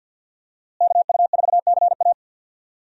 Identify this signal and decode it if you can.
Morse code, heard as a keyed tone, about 51 wpm, 700 Hz. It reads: KU4CA
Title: USA Call Signs - 5 Characters - Repeated 3x - 50wpm